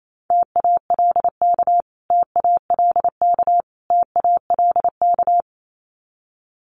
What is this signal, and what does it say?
Morse code, heard as a keyed tone, about 28 wpm, 700 Hz. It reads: TALK TALK TALK
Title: Top 500 Words Repeated 28wpm